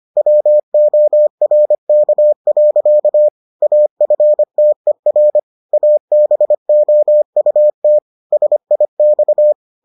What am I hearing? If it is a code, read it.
Morse code, heard as a keyed tone, about 25 wpm, 600 Hz. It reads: WORK. AFTER ABOUT SIX